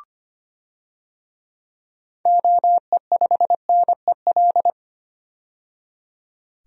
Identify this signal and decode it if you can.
Morse code, heard as a keyed tone, about 25 wpm, 700 Hz. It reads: OE5NEL
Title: DXCC-100 Call Signs - 6 Characters - 25wpm